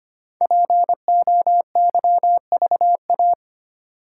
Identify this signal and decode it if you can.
Morse code, heard as a keyed tone, about 25 wpm, 700 Hz. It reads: POYVA
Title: Code Groups - 5 Letters - 25wpm